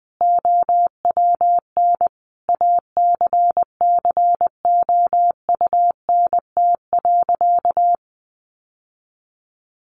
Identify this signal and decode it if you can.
Morse code, heard as a keyed tone, about 20 wpm, 700 Hz. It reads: OWN ACCOUNT.
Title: Sentences from Top 1400 Words 20wpm